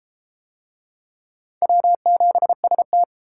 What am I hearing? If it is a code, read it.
Morse code, heard as a keyed tone, about 31 wpm, 700 Hz. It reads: W7ST